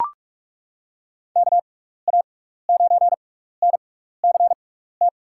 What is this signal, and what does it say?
Morse code, heard as a keyed tone, about 42 wpm, spaced as for 15 wpm, 700 Hz. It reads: KA9NCT
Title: USA Call Signs - 6 Characters - 6x Character Spacing - 45wpm